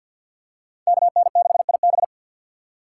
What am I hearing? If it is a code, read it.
Morse code, heard as a keyed tone, about 47 wpm, 700 Hz. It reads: KN6IB
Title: USA Call Signs - 5 Characters - 4x Character Spacing - 50wpm